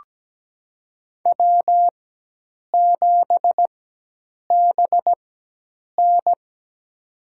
Morse code, heard as a keyed tone, about 17 wpm, spaced as for 8 wpm, 700 Hz. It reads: W7BN
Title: USA Call Signs - 4 Characters - 4x Character Spacing - 17wpm